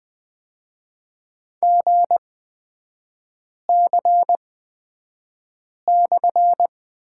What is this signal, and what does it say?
Morse code, heard as a keyed tone, about 20 wpm, spaced as for 5 wpm, 690 Hz. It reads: GC/
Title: 229 Lesson 37e - Letter x - Three Characters 20wpm